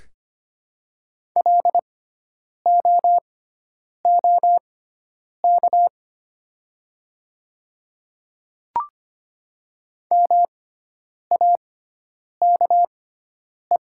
Morse code, heard as a keyed tone, about 25 wpm, spaced as for 9 wpm, 700 Hz. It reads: LOOK MAKE
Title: 6x Character Spacing - Top 100 Words - 25wpm